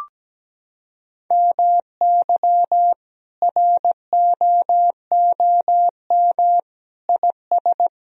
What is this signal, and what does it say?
Morse code, heard as a keyed tone, about 17 wpm, 700 Hz. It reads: MY ROOM IS
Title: Sentences from Top 1000 Words - Review Part 1 - 17wpm